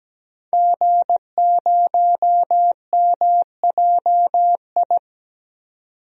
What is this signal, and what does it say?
Morse code, heard as a keyed tone, about 17 wpm, 700 Hz. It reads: G0MJI
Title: DXCC-100 Call Signs - 5 Characters - 17wpm